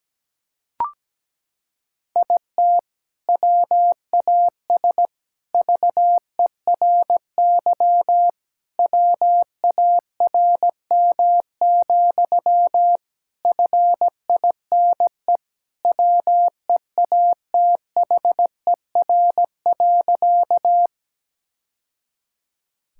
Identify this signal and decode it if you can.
Morse code, heard as a keyed tone, about 17 wpm, 700 Hz. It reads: IT WAS VERY WARM, FINE WEATHER.
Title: Sentences from Top 1000 Words - Review Part 5 - 17wpm